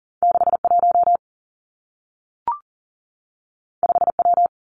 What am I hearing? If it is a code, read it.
Morse code, heard as a keyed tone, about 42 wpm, 700 Hz. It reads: 6 1 5 W
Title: Two Challenging Letter-Number Combinations 40wpm